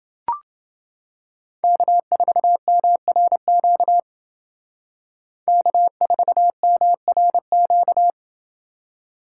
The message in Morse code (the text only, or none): K4MRQ K4MRQ